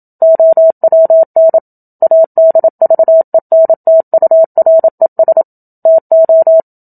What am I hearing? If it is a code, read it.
Morse code, heard as a keyed tone, about 27 wpm, 650 Hz. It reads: OWN ADVENTURES TO
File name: Book Two / 01-Chapter 1 - 27 WPM